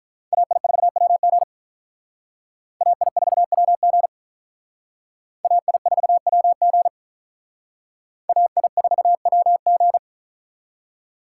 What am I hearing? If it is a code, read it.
Morse code, heard as a keyed tone, about 44 wpm, 700 Hz. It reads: AI4WG AI4WG AI4WG AI4WG